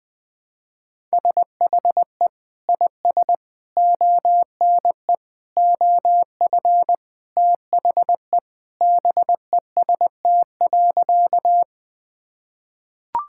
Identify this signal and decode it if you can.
Morse code, heard as a keyed tone, about 20 wpm, 700 Hz. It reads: SHE IS ONE OF THE BEST.
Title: Sentences with 2-Word Phrases - 20wpm